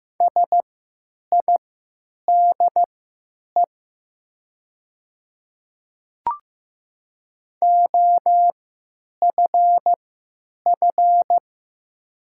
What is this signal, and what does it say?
Morse code, heard as a keyed tone, about 15 wpm, spaced as for 9 wpm, 700 Hz. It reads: SIDE OFF
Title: 3x Character Spacing - Top 500 Words - 15wpm